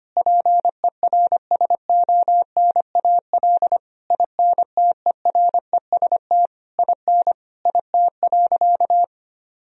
Morse code, heard as a keyed tone, about 25 wpm, 700 Hz. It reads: PERSONAL INTEREST IN IT.
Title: Sentences from Top 1000 Words - Review Part 4 - 25wpm